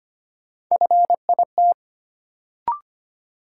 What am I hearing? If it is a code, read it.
Morse code, heard as a keyed tone, about 25 wpm, 700 Hz. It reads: FIT
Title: Sentences from Top 800 Words - New Word Drill - 25wpm